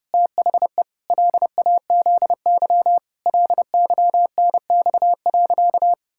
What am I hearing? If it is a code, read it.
Morse code, heard as a keyed tone, about 30 wpm, 700 Hz. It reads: THE LAZY LYNX.
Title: Pangram Challenge v2 - 30wpm